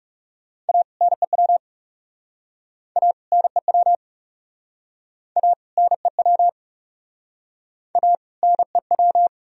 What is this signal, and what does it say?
Morse code, heard as a keyed tone, about 40 wpm, 700 Hz. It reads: A NEW A NEW A NEW A NEW